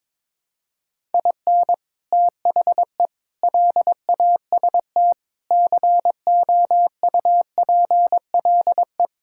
Morse code, heard as a keyed tone, about 22 wpm, 700 Hz. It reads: IN THE LAST COUPLE
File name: practice / 22wpm / Build A Phrase - 5-Word Phrases - 22wpm